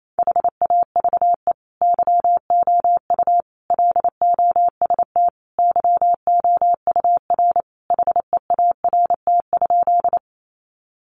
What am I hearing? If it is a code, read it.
Morse code, heard as a keyed tone, about 28 wpm, 700 Hz. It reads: HAVE YOU LOST YOUR HEART?
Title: Sentences from Top 1400 Words 28wpm